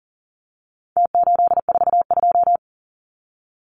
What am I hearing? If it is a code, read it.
Morse code, heard as a keyed tone, about 40 wpm, 700 Hz. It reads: T842
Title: Numbers - Contest Style - 4-Digit 40wpm